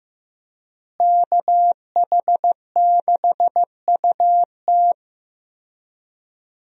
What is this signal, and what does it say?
Morse code, heard as a keyed tone, about 15 wpm, 700 Hz. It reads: KH6UT
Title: USA Call Signs - 5 Characters - Repeated 2x - 15wpm